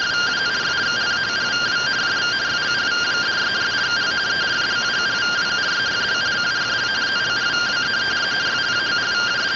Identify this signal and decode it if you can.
Two-tone teleprinter signal, about 170 Hz shift.